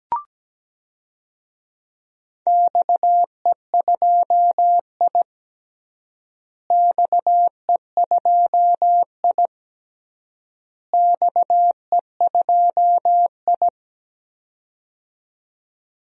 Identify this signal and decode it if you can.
Morse code, heard as a keyed tone, about 17 wpm, 700 Hz. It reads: XE2I XE2I XE2I
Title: DXCC-100 Call Signs - 4 Characters - Repeated 3x - 17wpm